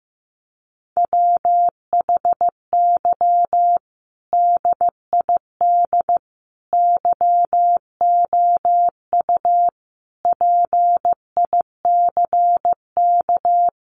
Morse code, heard as a keyed tone, about 15 wpm, 700 Hz. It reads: WHY DID YOU PICK
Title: Sentences from Top 1000 Words 15wpm